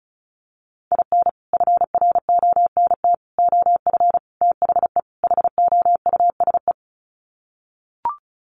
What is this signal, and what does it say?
Morse code, heard as a keyed tone, about 34 wpm, 700 Hz. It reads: IN FRONT OF THE HOUSE
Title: Sets of 5 Words Encore2 35wpm